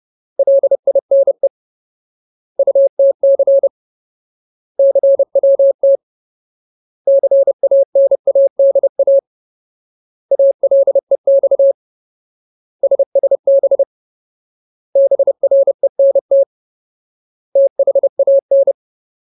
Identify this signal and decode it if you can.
Morse code, heard as a keyed tone, about 30 wpm, 550 Hz. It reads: LINE UTC CWT CANADA ALEX SSB BRENT THAN
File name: Ultimate Word List / 30WPM Ultimate